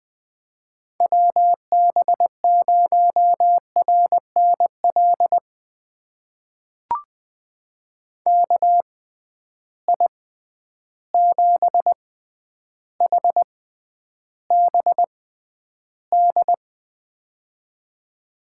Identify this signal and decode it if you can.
Morse code, heard as a keyed tone, about 20 wpm, spaced as for 10 wpm, 700 Hz. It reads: WB0RNL KI7HBD
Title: USA Call Signs - 6 Characters - 6x Character Spacing - 20wpm